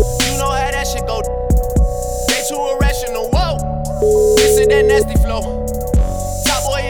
Ringback tone (Europe, 425 Hz), with noise.